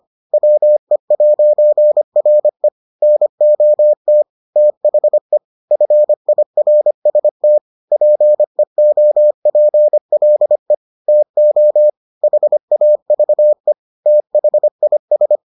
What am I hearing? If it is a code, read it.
Morse code, heard as a keyed tone, about 25 wpm, 600 Hz. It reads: WE'RE NOT THE FIRST PEOPLE TO HAVE THIS